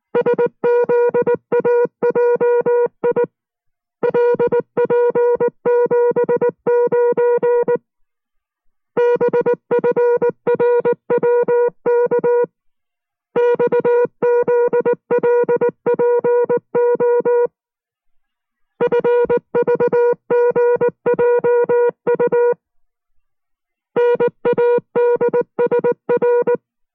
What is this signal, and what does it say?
Morse code, heard as a keyed tone, about 19 wpm, 480 Hz. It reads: SZAJI LP79 BFRWK XZLPO FVGJU NADSR